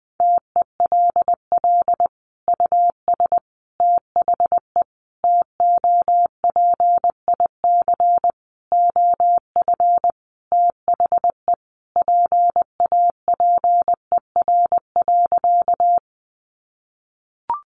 Morse code, heard as a keyed tone, about 19 wpm, 700 Hz. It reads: TELL US THE TOPIC OF THE PAPER.
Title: Sentences from Top 1200 Words 20wpm